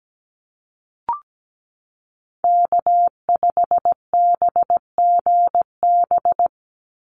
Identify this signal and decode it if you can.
Morse code, heard as a keyed tone, about 17 wpm, 700 Hz. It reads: K5BGB
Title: Call Signs Repeated 3x - 17wpm